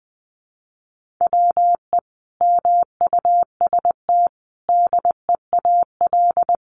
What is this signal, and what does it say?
Morse code, heard as a keyed tone, about 20 wpm, 700 Hz. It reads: WE MUST DEAL
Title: Sentences from Top 1400 Words 20wpm